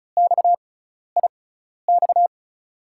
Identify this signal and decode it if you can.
Morse code, heard as a keyed tone, about 36 wpm, spaced as for 12 wpm, 700 Hz. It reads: XIX
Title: USA Call Signs - 5 Characters - 6x Character Spacing - 35wpm